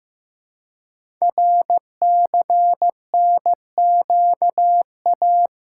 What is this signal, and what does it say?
Morse code, heard as a keyed tone, about 15 wpm, 700 Hz. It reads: RCNQA
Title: Code Groups - 5 Letters - 15wpm